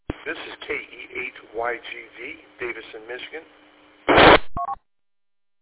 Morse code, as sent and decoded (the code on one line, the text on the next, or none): none